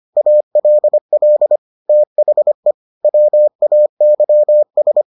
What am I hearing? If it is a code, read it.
Morse code, heard as a keyed tone, about 25 wpm, 600 Hz. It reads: ALL THE WAYS